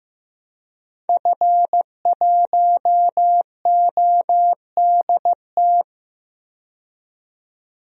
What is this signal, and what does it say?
Morse code, heard as a keyed tone, about 15 wpm, 700 Hz. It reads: F1ODT
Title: DXCC-100 Call Signs - 5 Characters - 15wpm